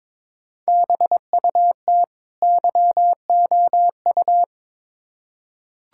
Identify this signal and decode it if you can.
Morse code, heard as a keyed tone, about 21 wpm, 700 Hz. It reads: BUT YOU
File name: Build A Phrase - 2-Word Phrases - 22wpm